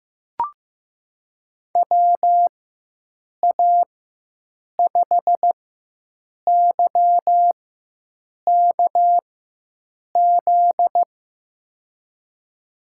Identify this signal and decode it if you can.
Morse code, heard as a keyed tone, about 15 wpm, spaced as for 7 wpm, 700 Hz. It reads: WA5YKZ
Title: USA Call Signs - 4x Character Spacing - 15wpm